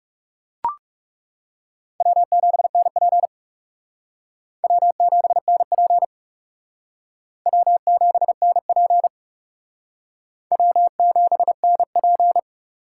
Morse code, heard as a keyed tone, about 37 wpm, 700 Hz. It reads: W7NP W7NP W7NP W7NP
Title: Speed Racing - USA 1x2 Call Signs 30wpm